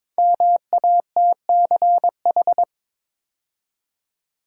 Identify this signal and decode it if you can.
Morse code, heard as a keyed tone, about 22 wpm, 700 Hz. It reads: MATCH
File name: 5 Letter Words - First 3 Letters Given - 22wpm